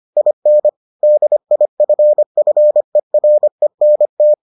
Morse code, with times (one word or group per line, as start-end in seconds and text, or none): 0.2-0.7 IN
1.0-4.3 DIFFERENT